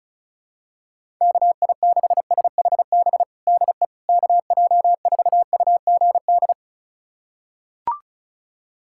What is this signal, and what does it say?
Morse code, heard as a keyed tone, about 36 wpm, 700 Hz. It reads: KI6SHB DE KJ4UGD
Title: USA Callsign DE Callsign 35wpm